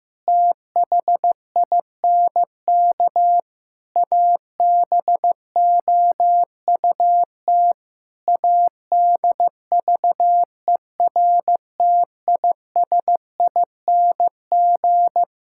Morse code, heard as a keyed tone, about 15 wpm, 700 Hz. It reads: THINK ABOUT ADVERTISING
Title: Sentences from Top 1000 Words - Review Part 6 - 15wpm